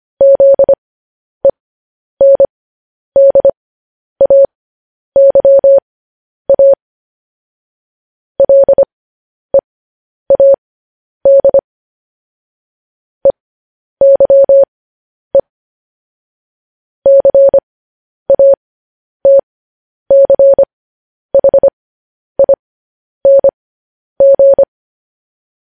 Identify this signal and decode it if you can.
Morse code, heard as a keyed tone, about 25 wpm, spaced as for 10 wpm, 550 Hz. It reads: ZENDAYA LEAD EYE CATCHING
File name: podcast / News headlines-1x-25x10